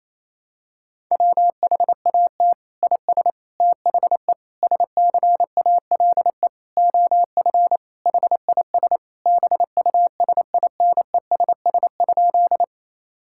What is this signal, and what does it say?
Morse code, heard as a keyed tone, about 27 wpm, 700 Hz. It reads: WHAT IS THE SCALE OF HIS BUSINESS?
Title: Sentences from Top 1000 Words - Review Part 5 - 28wpm